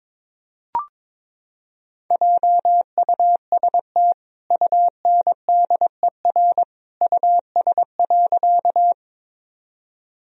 Morse code, heard as a keyed tone, about 21 wpm, 700 Hz. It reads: JUST UNDER US.
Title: Sentences from Top 1000 Words - Review Part 1 - 22wpm